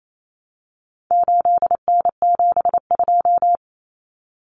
Morse code, heard as a keyed tone, about 28 wpm, 700 Hz. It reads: 8N72
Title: Common Cut Numbers - 4-Digit 28wpm